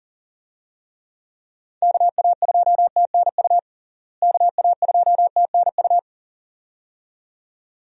Morse code, heard as a keyed tone, about 37 wpm, 700 Hz. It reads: KA2TNU KA2TNU
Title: USA Call Signs - 6 Characters - Repeated 2x - 40wpm